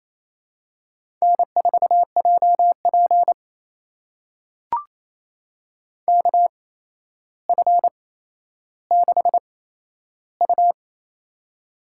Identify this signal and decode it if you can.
Morse code, heard as a keyed tone, about 28 wpm, spaced as for 12 wpm, 700 Hz. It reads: N4JP KF6U
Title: USA Call Signs - 4 Characters - 8x Character Spacing - 28wpm